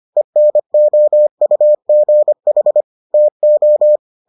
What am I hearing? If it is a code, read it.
Morse code, heard as a keyed tone, about 25 wpm, 600 Hz. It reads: ENOUGH TO